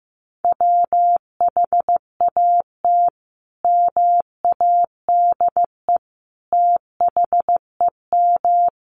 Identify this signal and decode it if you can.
Morse code, heard as a keyed tone, about 15 wpm, 700 Hz. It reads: WHAT MADE THEM
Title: Sentences from Top 1300 Words 15wpm